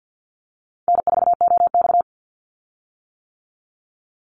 Morse code, heard as a keyed tone, about 50 wpm, 700 Hz. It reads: N4OX
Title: Easy Call Signs - 50wpm